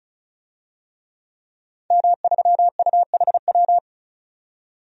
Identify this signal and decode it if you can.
Morse code, heard as a keyed tone, about 36 wpm, 700 Hz. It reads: M3UHW
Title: DXCC-100 Call Signs - 5 Characters - 35wpm